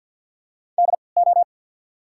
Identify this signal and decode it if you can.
Morse code, heard as a keyed tone, about 54 wpm, spaced as for 29 wpm, 700 Hz. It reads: DO
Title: 3x Character Spacing - Top 100 Words - 50wpm